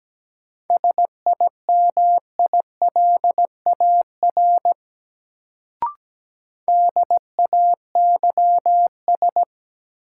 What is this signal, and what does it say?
Morse code, heard as a keyed tone, about 17 wpm, 700 Hz. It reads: SIMILAR DAYS
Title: Sentences from Top 1000 Words - Review Part 5 - New Word Drill - 17wpm